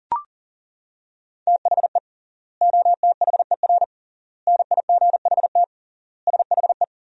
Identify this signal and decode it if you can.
Morse code, heard as a keyed tone, about 40 wpm, 700 Hz. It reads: THE OTHER NIGHT SHE
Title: 3x Word-Spacing - Sentences from Top 300 Words - 40wpm